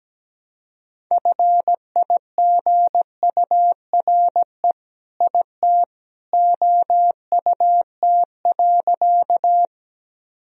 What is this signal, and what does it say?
Morse code, heard as a keyed tone, about 17 wpm, 700 Hz. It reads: FIGURE IT OUT.
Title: Sentences from Top 1000 Words - Review Part 2 - 17wpm